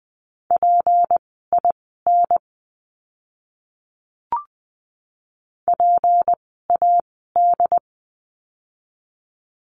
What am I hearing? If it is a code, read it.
Morse code, heard as a keyed tone, about 20 wpm, spaced as for 15 wpm, 700 Hz. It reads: PIN PAD
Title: 153 Lesson 25f - Letter p - Words 20wpm